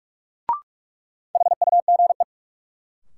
Morse code, heard as a keyed tone, about 46 wpm, 700 Hz. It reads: HUGE
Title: Sentences from Top 1100 Words - New Word Drill Alt - 45wpm